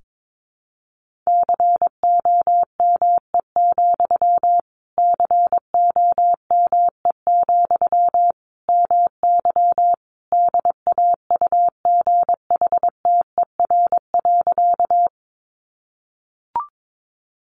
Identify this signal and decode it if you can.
Morse code, heard as a keyed tone, about 22 wpm, 700 Hz. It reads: COME, COME, MY DAUGHTER.
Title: Sentences from Top 1500 Words 22wpm